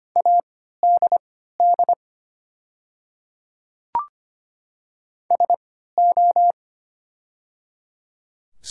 Morse code, heard as a keyed tone, about 25 wpm, spaced as for 14 wpm, 700 Hz. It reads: ADD SO
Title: 3x Character Spacing - Top 200 Words - 25wpm